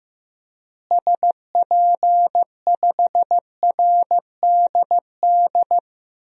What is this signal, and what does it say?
Morse code, heard as a keyed tone, about 15 wpm, 700 Hz. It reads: SP5RDD
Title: DXCC-100 Call Signs - 6 Characters - Repeated 2x - 15wpm